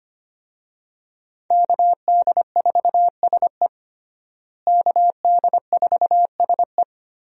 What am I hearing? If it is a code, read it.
Morse code, heard as a keyed tone, about 25 wpm, 700 Hz. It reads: KD4SE KD4SE
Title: USA Call Signs - 5 Characters - Repeated 2x - 25wpm